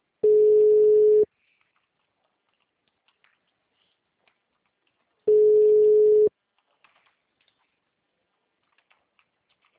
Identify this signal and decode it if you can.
Ringback tone (Europe, 425 Hz).